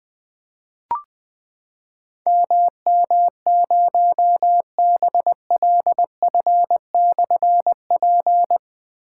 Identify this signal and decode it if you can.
Morse code, heard as a keyed tone, about 20 wpm, 700 Hz. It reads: MM0BLF/P
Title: Difficult Call Signs 20wpm